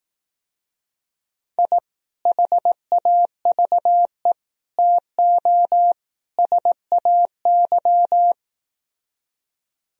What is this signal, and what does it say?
Morse code, heard as a keyed tone, about 18 wpm, 700 Hz. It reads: I HAVE TO SAY
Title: Speed-Racing - Sets of 4 Words 15wpm